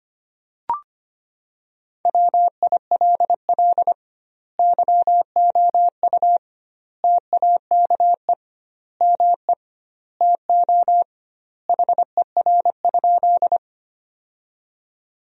Morse code, heard as a keyed tone, about 25 wpm, 700 Hz. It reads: WILL YOU TAKE ME TO HER?
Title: 2x Word-Spacing - Sentences from Top 200 Words - 25wpm